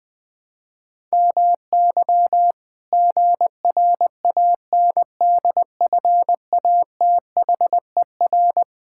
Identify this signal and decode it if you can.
Morse code, heard as a keyed tone, about 21 wpm, 700 Hz. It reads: MY GRANDFATHER
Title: Pangram Challenge v2 - 20wpm